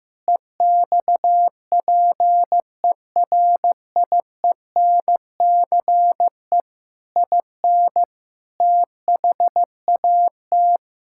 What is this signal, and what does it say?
Morse code, heard as a keyed tone, about 15 wpm, 700 Hz. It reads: EXPERIENCE IN THAT
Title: Sentences from Top 1000 Words - Review Part 6 - 15wpm